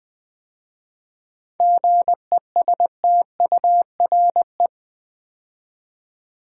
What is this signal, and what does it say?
Morse code, heard as a keyed tone, about 20 wpm, 700 Hz. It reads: GESTURE
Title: 7 Letter Words - First 3 Letters Given - 20wpm